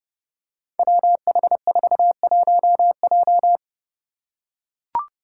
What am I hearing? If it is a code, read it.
Morse code, heard as a keyed tone, about 30 wpm, 700 Hz. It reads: WH41J
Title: Five Challenging Letter-Number Combinations 30wpm